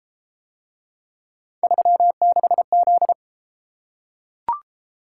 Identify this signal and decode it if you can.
Morse code, heard as a keyed tone, about 31 wpm, 700 Hz. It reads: 36Z